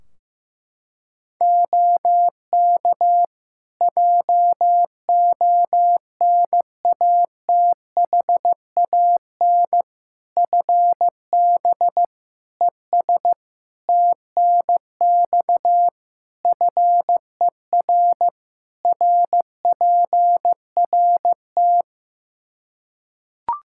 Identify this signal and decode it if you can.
Morse code, heard as a keyed tone, about 15 wpm, 700 Hz. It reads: OK JONATHAN FB ES TNX FER RPRT
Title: Practice QSO - Exchange 2 - Info - 15wpm